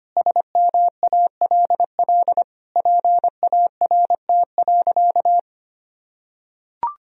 Morse code, heard as a keyed tone, about 25 wpm, 700 Hz. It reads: SMALL PART.
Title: Sentences from Top 1000 Words - Review Part 1 - 25wpm